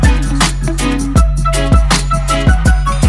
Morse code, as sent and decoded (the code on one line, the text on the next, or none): none